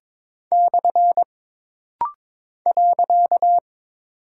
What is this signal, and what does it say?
Morse code, heard as a keyed tone, about 22 wpm, 700 Hz. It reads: / .
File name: practice / 22wpm / Single Punctuation-Prosign - Rapid-Fire - 22wpm